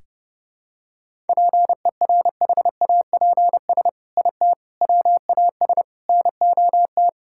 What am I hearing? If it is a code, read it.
Morse code, heard as a keyed tone, about 32 wpm, 700 Hz. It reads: PERHAPS IT WAS NOT
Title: Sentences from Top 1000 Words - Review Part 5 - 30wpm